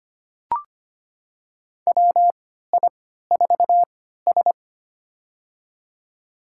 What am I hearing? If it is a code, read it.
Morse code, heard as a keyed tone, about 25 wpm, spaced as for 14 wpm, 700 Hz. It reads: WI4S
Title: USA Call Signs - 4 Characters - 3x Character Spacing - 25wpm